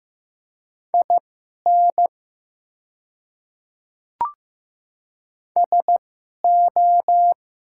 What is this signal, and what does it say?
Morse code, heard as a keyed tone, about 15 wpm, spaced as for 11 wpm, 690 Hz. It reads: IN SO
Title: 2x Character Spacing - Top 100 Words - 15wpm